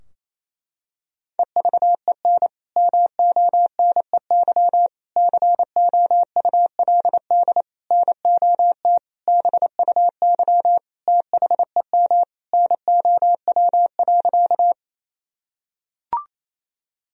Morse code, heard as a keyed tone, about 28 wpm, 700 Hz. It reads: EVEN MONEY COULD NOT BUY THEM NOW.
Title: Sentences from Top 1000 Words - Review Part 4 - 28wpm